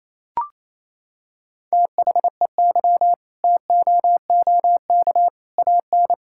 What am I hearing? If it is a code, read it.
Morse code, heard as a keyed tone, about 28 wpm, 700 Hz. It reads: THEY TOOK AN
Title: Sentences from Top 1000 Words - Review Part 6 - 28wpm